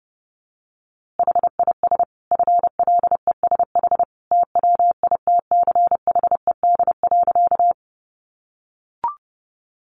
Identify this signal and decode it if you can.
Morse code, heard as a keyed tone, about 30 wpm, 700 Hz. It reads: HIS FLESH TWITCHED.
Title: A Sound Of Thunder 30wpm